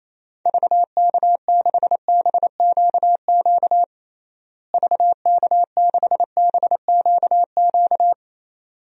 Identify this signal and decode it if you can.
Morse code, heard as a keyed tone, about 28 wpm, 700 Hz. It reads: VK6BQQ VK6BQQ
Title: DXCC-100 Call Signs - 6 Characters - Repeated 2x - 28wpm